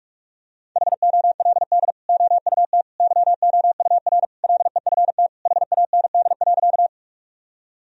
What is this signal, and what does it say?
Morse code, heard as a keyed tone, about 43 wpm, 700 Hz. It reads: HOLD OUT YOUR LEFT HAND.
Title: Sentences from Top 1000 Words - Review Part 2 - 45wpm